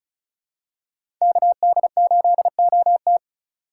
Morse code, heard as a keyed tone, about 36 wpm, 700 Hz. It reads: KD8OT
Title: USA Call Signs - 2x Character Spacing - 35wpm